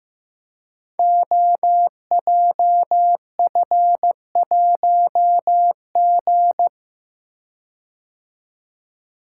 Morse code, heard as a keyed tone, about 15 wpm, 700 Hz. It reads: OJF1G
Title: Code Groups - 5 Letter:Numbers - 15wpm